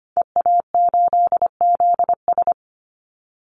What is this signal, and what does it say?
Morse code, heard as a keyed tone, about 25 wpm, 700 Hz. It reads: EA8ZS
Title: Call Signs - 25wpm